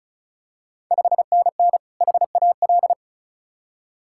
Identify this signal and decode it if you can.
Morse code, heard as a keyed tone, about 36 wpm, 700 Hz. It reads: 5NN HAL
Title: Ontario QSO Party Exchange 35wpm